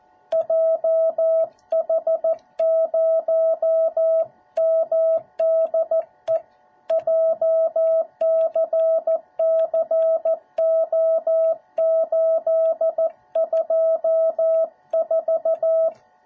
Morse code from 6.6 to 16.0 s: JCCO824